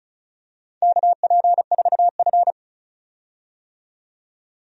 Morse code, heard as a keyed tone, about 36 wpm, 700 Hz. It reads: KP4F
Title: DXCC-100 Call Signs - 4 Characters - 35wpm